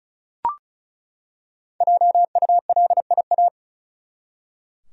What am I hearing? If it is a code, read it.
Morse code, heard as a keyed tone, about 35 wpm, 700 Hz. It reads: JULIA
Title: USA Common Names - Max 5 Letters - 35wpm